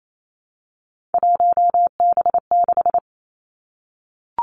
Morse code, heard as a keyed tone, about 28 wpm, 700 Hz. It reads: 1B6
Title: Three Challenging Letter-Number Combinations 28wpm